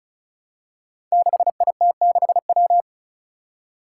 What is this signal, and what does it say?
Morse code, heard as a keyed tone, about 34 wpm, 700 Hz. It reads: 6IT6W
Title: Five Challenging Letter-Number Combinations 35wpm